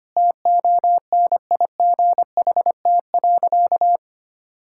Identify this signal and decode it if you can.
Morse code, heard as a keyed tone, about 25 wpm, 700 Hz. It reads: TONIGHT.
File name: practice / 25wpm / Sentences with 2-Word Phrases - 25wpm